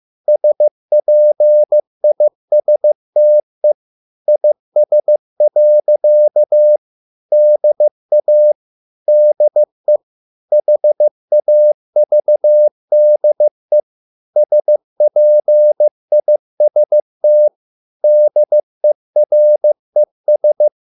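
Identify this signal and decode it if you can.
Morse code, heard as a keyed tone, about 15 wpm, 600 Hz. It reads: SPISTE IS. DA DE HAVDE SPIST DERES
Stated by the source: Morse code at a slow 15 wpm from CW med Gnister.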